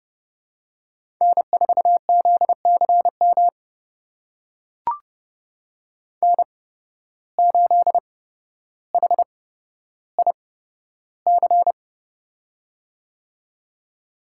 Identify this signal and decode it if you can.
Morse code, heard as a keyed tone, about 30 wpm, spaced as for 13 wpm, 700 Hz. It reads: N4ZCM N8HIC
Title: USA Call Signs - 5 Characters - 8x Character Spacing - 30wpm